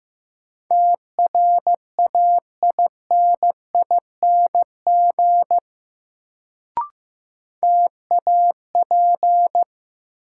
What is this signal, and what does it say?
Morse code, heard as a keyed tone, about 15 wpm, 700 Hz. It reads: TRAINING TAP
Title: Sentences from Top 1400 Words - New Word Drill Alt - 15wpm